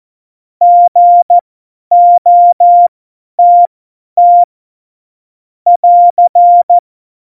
Morse code, heard as a keyed tone, about 14 wpm, spaced as for 10 wpm, 700 Hz. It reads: GOTT <AR>
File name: Morsekurs / Witze / Witz 27. Oktober 2020, 50 F70